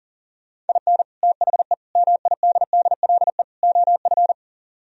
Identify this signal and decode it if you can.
Morse code, heard as a keyed tone, about 52 wpm, 700 Hz. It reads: IN THE MIDDLE OF